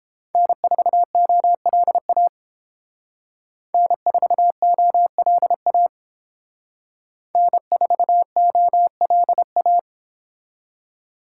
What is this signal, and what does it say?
Morse code, heard as a keyed tone, about 30 wpm, 700 Hz. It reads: N4OLA N4OLA N4OLA